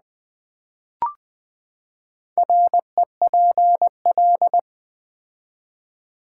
Morse code, heard as a keyed tone, about 20 wpm, 700 Hz. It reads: REPL
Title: Build A Word - 4 Letters At A Time - 11-Letter Words - 20wpm